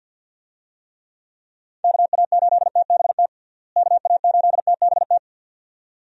Morse code, heard as a keyed tone, about 52 wpm, 700 Hz. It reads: KA8TBT KA8TBT
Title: USA Call Signs - 6 Characters - Repeated 2x - 50wpm